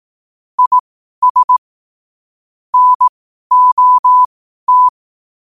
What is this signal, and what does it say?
Morse code, heard as a keyed tone, about 18 wpm, spaced as for 12 wpm, 1000 Hz. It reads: IS NOT
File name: qotdfiles / 12wpm